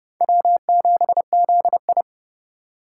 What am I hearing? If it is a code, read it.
Morse code, heard as a keyed tone, about 32 wpm, 700 Hz. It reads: W7ZI
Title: USA Call Signs - 4 Characters - 2x Character Spacing - 30wpm